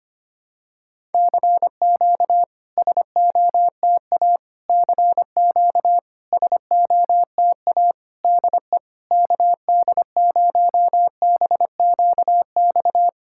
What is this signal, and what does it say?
Morse code, heard as a keyed tone, about 25 wpm, 700 Hz. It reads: CQ SOTA CQ SOTA DE KD0BQX
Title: SOTA QSO Practice 25wpm